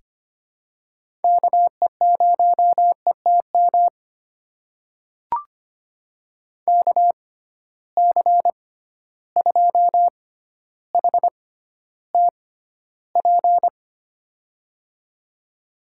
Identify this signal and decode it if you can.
Morse code, heard as a keyed tone, about 25 wpm, spaced as for 13 wpm, 700 Hz. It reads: KE0ETM KC2HTP
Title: USA Call Signs - 6 Characters - 6x Character Spacing - 25wpm